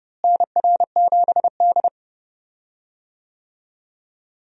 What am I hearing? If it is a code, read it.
Morse code, heard as a keyed tone, about 31 wpm, 700 Hz. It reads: NR7D
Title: USA Call Signs - 4 Characters - Repeated 3x - 30wpm